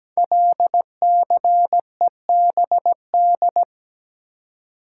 Morse code, heard as a keyed tone, about 17 wpm, 700 Hz. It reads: LCEBD
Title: Code Groups - 5 Letters - 17wpm